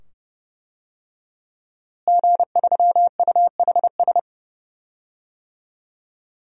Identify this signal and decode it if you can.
Morse code, heard as a keyed tone, about 30 wpm, 700 Hz. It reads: G3UHS
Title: DXCC-100 Call Signs - 5 Characters - Repeated 2x - 30wpm